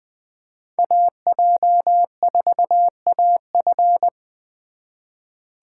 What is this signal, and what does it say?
Morse code, heard as a keyed tone, about 24 wpm, 700 Hz. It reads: AJ4AF
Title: Speed Racing - USA 2x2 Call Signs 20wpm